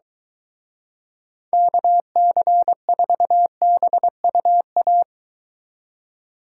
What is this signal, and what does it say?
Morse code, heard as a keyed tone, about 23 wpm, 700 Hz. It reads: KC4BUA